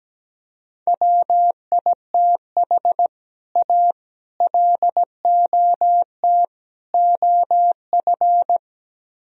Morse code, heard as a keyed tone, about 17 wpm, 700 Hz. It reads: WITH A LOT OF
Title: Speed-Racing - Sets of 4 Words 17wpm